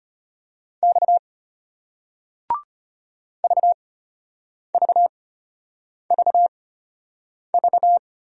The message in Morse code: X VVVV

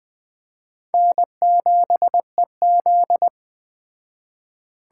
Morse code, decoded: N7EZ

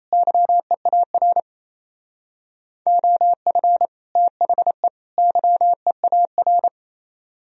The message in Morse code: YEAR OF THE YEAR

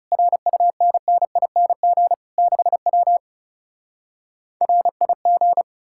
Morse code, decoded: RUNNING 6W RIG

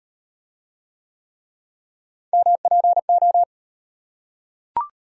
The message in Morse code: MPO

